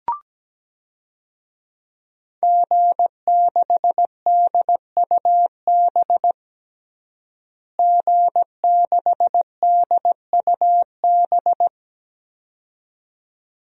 Morse code, heard as a keyed tone, about 17 wpm, 700 Hz. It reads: G6DUB G6DUB
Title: DXCC-100 Call Signs - 5 Characters - Repeated 2x - 17wpm